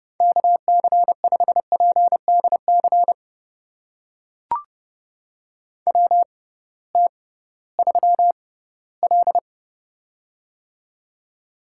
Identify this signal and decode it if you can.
Morse code, heard as a keyed tone, about 30 wpm, spaced as for 17 wpm, 700 Hz. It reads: KC5PDC WT3L
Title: USA Call Signs - 6x Character Spacing - 30wpm